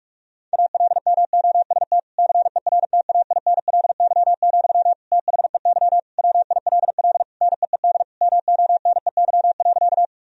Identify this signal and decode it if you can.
Morse code, heard as a keyed tone, about 46 wpm, 700 Hz. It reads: ALMOST CERTAINLY, THEY WILL NEED MONEY.
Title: Sentences from Top 1000 Words - Review Part 4 - 45wpm